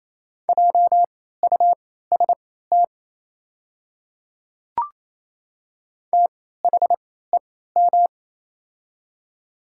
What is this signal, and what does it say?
Morse code, heard as a keyed tone, about 28 wpm, spaced as for 16 wpm, 700 Hz. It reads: JUST THEM
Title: 3x Character Spacing - Top 200 Words - 28wpm